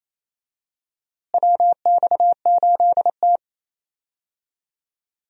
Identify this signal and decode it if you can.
Morse code, heard as a keyed tone, about 27 wpm, 700 Hz. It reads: WX8T
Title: USA Call Signs - 4 Characters - Repeated 3x - 28wpm